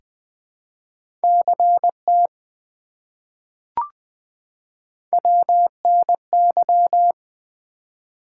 Morse code, heard as a keyed tone, about 20 wpm, 700 Hz. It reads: CT WNY
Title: ARRL Sections v2 - 20wpm